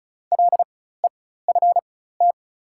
Morse code, heard as a keyed tone, about 34 wpm, spaced as for 16 wpm, 700 Hz. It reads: LEFT
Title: 4x Character Spacing - Top 500 Words - 35wpm